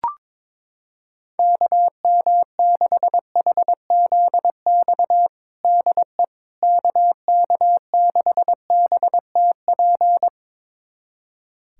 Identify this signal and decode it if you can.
Morse code, heard as a keyed tone, about 22 wpm, 700 Hz. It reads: KM6HZX DE KK6BTP
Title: USA Callsign DE Callsign 22wpm